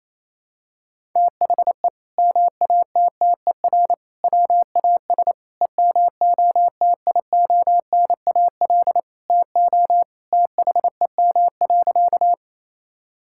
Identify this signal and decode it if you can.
Morse code, heard as a keyed tone, about 27 wpm, 700 Hz. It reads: THE MATTER WAS EMOTIONAL TO THEM.
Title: Sentences from Top 1000 Words - Review Part 5 - 28wpm